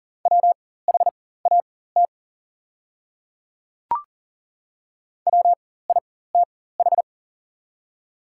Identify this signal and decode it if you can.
Morse code, heard as a keyed tone, about 40 wpm, spaced as for 19 wpm, 690 Hz. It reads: WHAT WITH
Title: 4x Character Spacing - Top 100 Words - 40wpm